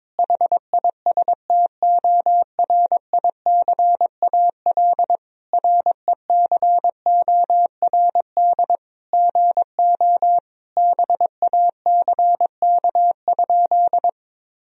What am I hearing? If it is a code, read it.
Morse code, heard as a keyed tone, about 22 wpm, 700 Hz. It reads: HISTORICAL RECORD GO BACK?
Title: Sentences from Top 1000 Words - Review Part 4 - 22wpm